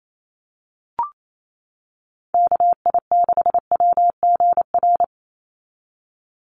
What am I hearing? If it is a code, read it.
Morse code, heard as a keyed tone, about 28 wpm, 700 Hz. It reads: KI6WGR
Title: USA Call Signs 28wpm